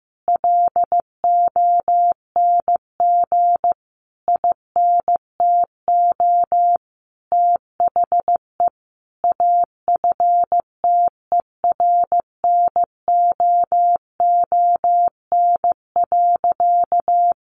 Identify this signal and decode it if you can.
Morse code, heard as a keyed tone, about 15 wpm, 700 Hz. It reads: LONG INTO THE AFTERNOON.
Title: Sentences from Top 1500 Words 15wpm